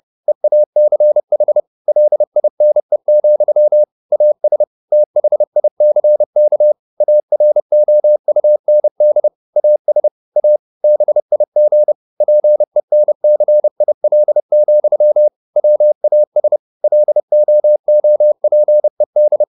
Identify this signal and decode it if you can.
Morse code, heard as a keyed tone, about 30 wpm, 600 Hz. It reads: EACH LINE, AS THICK AROUND AS A BIG PENCIL, WAS LOOPED
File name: the old man and the sea / the old man and the sea 1 30wpm